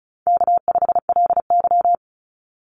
Morse code, heard as a keyed tone, about 34 wpm, 700 Hz. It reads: K5LY
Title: Easy Call Signs Repeated 3x - 35wpm